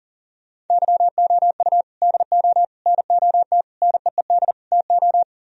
Morse code, heard as a keyed tone, about 52 wpm, 700 Hz. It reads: YOU DO NOT NEED TO